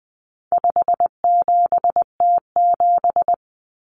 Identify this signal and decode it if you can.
Morse code, heard as a keyed tone, about 20 wpm, 700 Hz. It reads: 57T7
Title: Common Cut Numbers - 4-Digit 20wpm